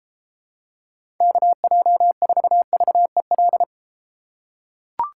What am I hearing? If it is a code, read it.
Morse code, heard as a keyed tone, about 32 wpm, 700 Hz. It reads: KJ4VEL